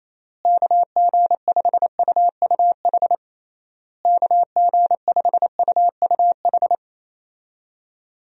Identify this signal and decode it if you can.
Morse code, heard as a keyed tone, about 28 wpm, 700 Hz. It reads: KG5UUH KG5UUH
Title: USA Call Signs - 6 Characters - Repeated 3x - 28wpm